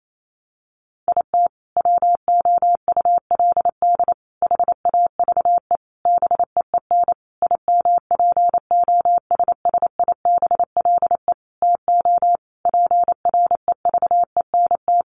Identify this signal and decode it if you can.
Morse code, heard as a keyed tone, about 28 wpm, 700 Hz. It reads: IT WOULD HAVE BEEN IMPOSSIBLE TO PREVENT
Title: Sentences from Top 1100 Words 28wpm